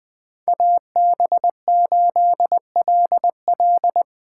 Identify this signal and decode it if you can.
Morse code, heard as a keyed tone, about 21 wpm, 700 Hz. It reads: AB8LL
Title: 2023 POTA Hunter Call Signs - 20wpm